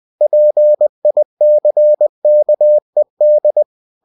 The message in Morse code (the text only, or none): PICKED